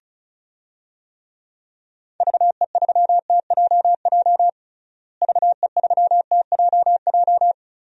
Morse code, heard as a keyed tone, about 34 wpm, 700 Hz. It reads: VE3TJJ VE3TJJ
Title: DXCC-100 Call Signs - 6 Characters - Repeated 3x - 35wpm